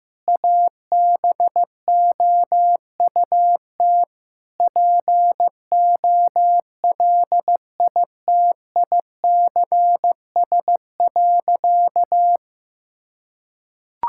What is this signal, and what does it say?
Morse code, heard as a keyed tone, about 16 wpm, 700 Hz. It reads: ABOUT POLITICS.
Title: Sentences from Top 1000 Words - Review Part 6 - 15wpm